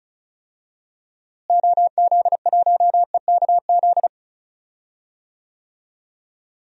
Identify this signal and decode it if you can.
Morse code, heard as a keyed tone, about 36 wpm, 700 Hz. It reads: OZ1EKZ
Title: DXCC-100 Call Signs - 6 Characters - Repeated 3x - 35wpm